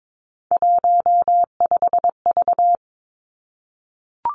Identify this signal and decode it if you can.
Morse code, heard as a keyed tone, about 22 wpm, 700 Hz. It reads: 15V
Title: Three Challenging Letter-Number Combinations 22wpm